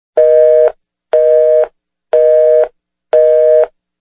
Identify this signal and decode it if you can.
Busy tone (North America), 480 + 620 Hz, with noise.
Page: phone-busy-tone